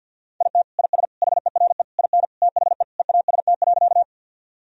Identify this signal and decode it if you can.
Morse code, heard as a keyed tone, about 50 wpm, 700 Hz. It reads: IT IS HERE IN THE EAST.
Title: Sentences from Top 1000 Words - Review Part 3 - 50wpm